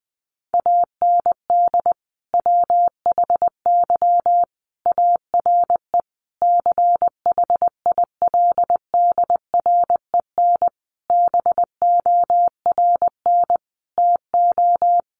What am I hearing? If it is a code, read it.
Morse code, heard as a keyed tone, about 20 wpm, 700 Hz. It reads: AND WHY ARE CHILDREN BORN TO
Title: Sentences from Top 1000 Words 20wpm